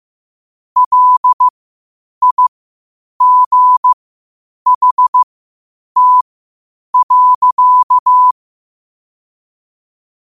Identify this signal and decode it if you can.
Morse code, heard as a keyed tone, about 15 wpm, spaced as for 8 wpm, 1000 Hz. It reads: LIGHT.
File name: qotdfiles / 8wpm